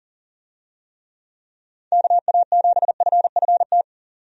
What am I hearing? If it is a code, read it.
Morse code, heard as a keyed tone, about 38 wpm, 700 Hz. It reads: KA7FFT
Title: USA Call Signs - 6 Characters - 40wpm